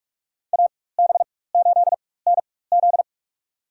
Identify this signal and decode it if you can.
Morse code, heard as a keyed tone, about 46 wpm, spaced as for 21 wpm, 700 Hz. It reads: AB8NZ
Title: USA Call Signs - 5 Characters - 4x Character Spacing - 45wpm